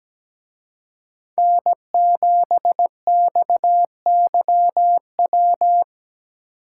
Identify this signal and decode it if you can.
Morse code, heard as a keyed tone, about 20 wpm, 700 Hz. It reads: N7XYW
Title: Speed Racing - USA 1x3 Call Signs 17wpm